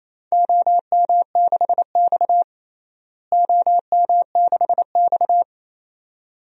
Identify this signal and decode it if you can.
Morse code, heard as a keyed tone, about 28 wpm, 700 Hz. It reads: OM6X OM6X
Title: DXCC-100 Call Signs - 4 Characters - Repeated 2x - 28wpm